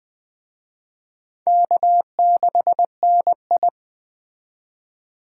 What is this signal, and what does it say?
Morse code, heard as a keyed tone, about 20 wpm, 700 Hz. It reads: K6NI
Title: Speed Racing - USA 1x2 Call Signs 17wpm